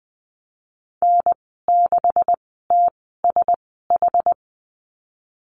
Morse code, heard as a keyed tone, about 20 wpm, spaced as for 15 wpm, 700 Hz. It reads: N6TSH
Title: 213 Lesson 34g - Number 8 - Call Signs 20wpm